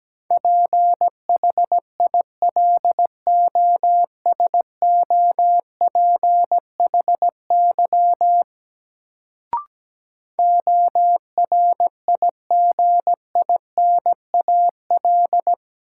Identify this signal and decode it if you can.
Morse code, heard as a keyed tone, about 17 wpm, 700 Hz. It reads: PHILOSOPHY ORIGINAL
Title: Sentences from Top 900 Words - New Word Drill - 17wpm